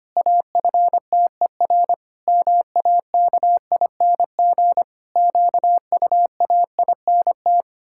Morse code, heard as a keyed tone, about 25 wpm, 700 Hz. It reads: AFTER MAKING QUAINT
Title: Pangram Challenge v2 - 25wpm